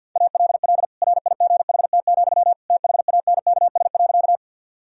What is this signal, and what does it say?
Morse code, heard as a keyed tone, about 50 wpm, 700 Hz. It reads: ALL RIGHT, THANKS.
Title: Sentences from Top 1000 Words - Review Part 3 - 50wpm